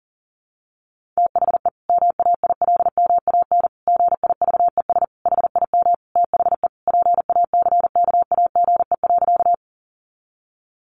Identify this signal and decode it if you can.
Morse code, heard as a keyed tone, about 40 wpm, 700 Hz. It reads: THE MAILMAN GIVES HIM THE PACKAGE.
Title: Additional Easy Phrases 40wpm